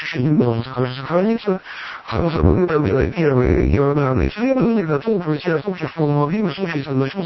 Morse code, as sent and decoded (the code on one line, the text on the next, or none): none